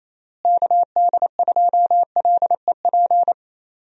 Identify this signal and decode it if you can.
Morse code, heard as a keyed tone, about 28 wpm, 700 Hz. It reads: KD2LEP
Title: USA Call Signs - 6 Characters - Repeated 3x - 28wpm